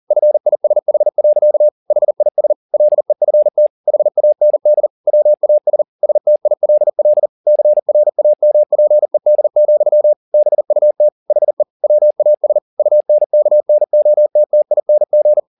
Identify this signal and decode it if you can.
Morse code, heard as a keyed tone, about 40 wpm, 600 Hz. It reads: FISH. HIS LEFT HAND WAS STILL CRAMPED, BUT HE WAS UNKNOTTING